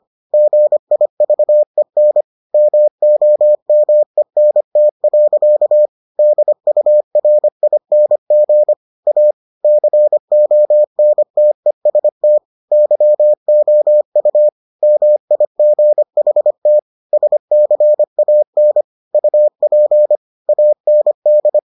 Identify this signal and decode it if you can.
Morse code, heard as a keyed tone, about 25 wpm, 600 Hz. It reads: GIVEN MOMENT. DURING A CONTEST YOU MIGHT SCAN UP AND